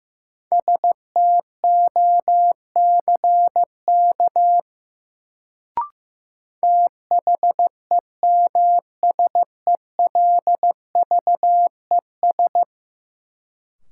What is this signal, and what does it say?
Morse code, heard as a keyed tone, about 15 wpm, 700 Hz. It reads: STOCK THEMSELVES
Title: Sentences from Top 800 Words - New Word Drill - 15wpm